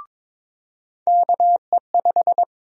Morse code, heard as a keyed tone, about 22 wpm, 700 Hz. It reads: KE5
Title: USA Call Sign Prefixes - 22wpm